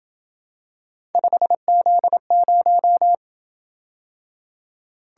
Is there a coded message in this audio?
Yes: Morse code from 1.1 to 3.1 s, heard as a keyed tone, about 26 wpm, 700 Hz. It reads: 5Z0